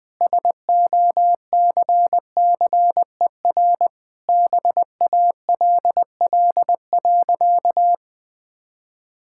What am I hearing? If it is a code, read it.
Morse code, heard as a keyed tone, about 20 wpm, 700 Hz. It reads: SOCCER BALL.
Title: Easy Sentences with Hints - 20wpm